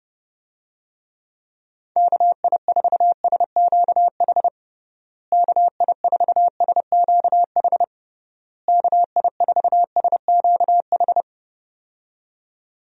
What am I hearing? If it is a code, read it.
Morse code, heard as a keyed tone, about 32 wpm, 700 Hz. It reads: KI4SQH KI4SQH KI4SQH
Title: USA Call Signs - 6 Characters - Repeated 3x - 30wpm